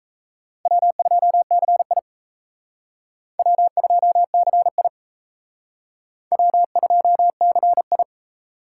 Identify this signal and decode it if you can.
Morse code, heard as a keyed tone, about 36 wpm, 700 Hz. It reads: W2CI W2CI W2CI